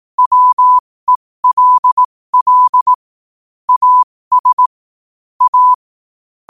Morse code, heard as a keyed tone, about 18 wpm, spaced as for 15 wpm, 1000 Hz. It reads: WELL AS A
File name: qotdfiles / 15wpm